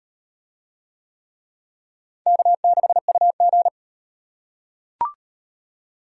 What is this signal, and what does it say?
Morse code, heard as a keyed tone, about 38 wpm, 700 Hz. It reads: K6UG